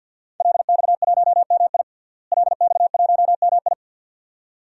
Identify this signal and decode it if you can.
Morse code, heard as a keyed tone, about 49 wpm, 700 Hz. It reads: LX1MI LX1MI
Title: DXCC-100 Call Signs - 5 Characters - Repeated 2x - 50wpm